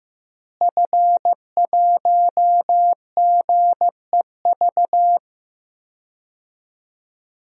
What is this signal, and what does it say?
Morse code, heard as a keyed tone, about 15 wpm, 700 Hz. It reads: F1GEV
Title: DXCC-100 Call Signs - 5 Characters - Repeated 3x - 15wpm